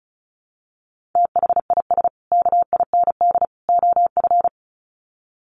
Morse code, heard as a keyed tone, about 34 wpm, 700 Hz. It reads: THIS KIND OF
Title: Sets of 3 Words 35wpm